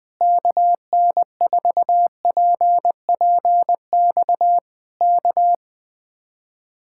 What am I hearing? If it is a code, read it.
Morse code, heard as a keyed tone, about 20 wpm, 700 Hz. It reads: KN4PPX K
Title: Speed-Racing Common QSO Phrases - 17wpm